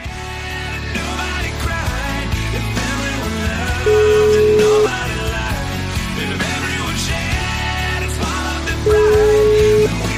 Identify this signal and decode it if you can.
Ringback tone (Europe, 425 Hz), with noise.